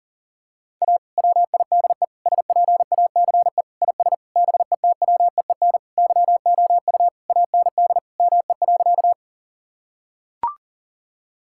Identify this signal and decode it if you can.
Morse code, heard as a keyed tone, about 40 wpm, 700 Hz. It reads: A WIDE SPACE IS BETWEEN YOU AND ME.
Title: Sentences from Top 1000 Words - Review Part 6 - 40wpm